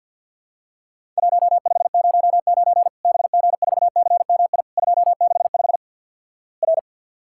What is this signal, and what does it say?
Morse code, heard as a keyed tone, about 53 wpm, 700 Hz. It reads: 1509 BG4CMI 265 R
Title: Worked All Europe DX Contest - QTC Exchange - 50wpm